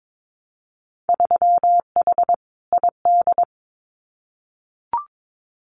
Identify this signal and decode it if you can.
Morse code, heard as a keyed tone, about 22 wpm, 700 Hz. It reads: 3H ID
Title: Winter Field Day Exchanges Encore 22wpm